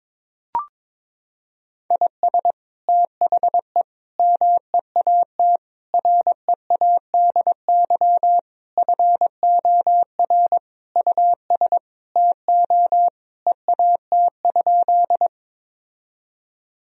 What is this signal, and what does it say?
Morse code, heard as a keyed tone, about 22 wpm, 700 Hz. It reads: IS THE MEAT READY FOR US TO EAT?
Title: Sentences from Top 1000 Words - Review Part 3 - 22wpm